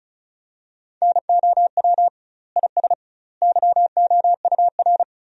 Morse code, heard as a keyed tone, about 36 wpm, 700 Hz. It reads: NOW IS YOUR
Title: 2x Word-Spacing - Sentences from Top 100 Words - 35wpm